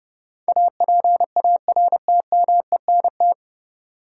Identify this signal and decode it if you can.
Morse code, heard as a keyed tone, about 30 wpm, 700 Hz. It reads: APARTMENT
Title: Sentences from Top 900 Words - New Word Drill - 30wpm